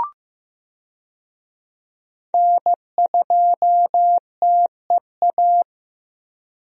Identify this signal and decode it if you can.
Morse code, heard as a keyed tone, about 15 wpm, 700 Hz. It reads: N2TEA
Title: USA Call Signs - 5 Characters - Repeated 2x - 15wpm